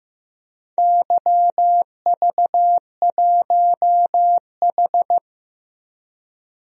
Morse code, heard as a keyed tone, about 15 wpm, 700 Hz. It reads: YV1H
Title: DXCC-100 Call Signs - 4 Characters - Repeated 2x - 15wpm